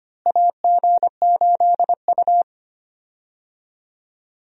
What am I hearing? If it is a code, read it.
Morse code, heard as a keyed tone, about 25 wpm, 700 Hz. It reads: AG8U
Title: USA Call Signs - 4 Characters - Repeated 3x - 25wpm